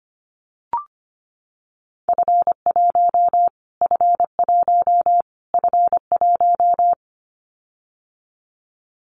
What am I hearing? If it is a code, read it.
Morse code, heard as a keyed tone, about 25 wpm, 700 Hz. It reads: F1 F1 F1
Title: Callsign Prefixes for Top 25 DXCC - Repeated 3x - 25wpm